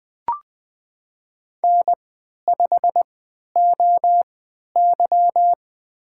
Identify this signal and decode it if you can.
Morse code, heard as a keyed tone, about 20 wpm, spaced as for 11 wpm, 700 Hz. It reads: N5OY
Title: USA Call Signs - 4 Characters - 3x Character Spacing - 20wpm